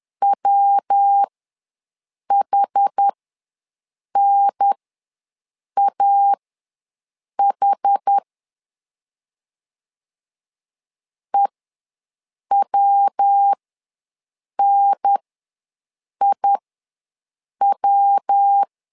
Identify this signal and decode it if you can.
Morse code, heard as a keyed tone, about 10 wpm, spaced as for 6 wpm, 800 Hz. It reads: WHNAH EWNIW
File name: LEZIONI / 3 MWRO / 3 MW ws 10wpm 2213